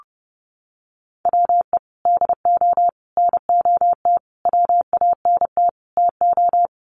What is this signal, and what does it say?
Morse code, heard as a keyed tone, about 30 wpm, 700 Hz. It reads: WE DO NOT WANT TO
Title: Sets of 5 Words 30wpm